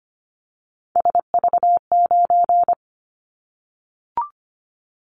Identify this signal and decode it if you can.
Morse code, heard as a keyed tone, about 25 wpm, 700 Hz. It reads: SV9
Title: Callsign Prefixes for Top 100 DXCC - 25wpm